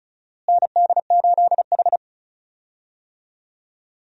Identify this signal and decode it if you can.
Morse code, heard as a keyed tone, about 36 wpm, 700 Hz. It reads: ND8H
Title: USA Call Signs - 4 Characters - Repeated 3x - 35wpm